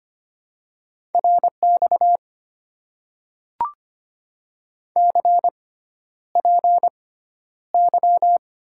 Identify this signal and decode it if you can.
Morse code, heard as a keyed tone, about 25 wpm, spaced as for 11 wpm, 700 Hz. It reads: RX CPY
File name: Top 90 QSO Elements - 6x Character Spacing - 25wpm